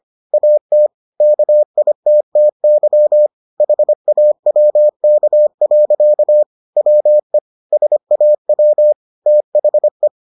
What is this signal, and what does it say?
Morse code, heard as a keyed tone, about 25 wpm, 600 Hz. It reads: AT KITTY HAWK. WE SAW THE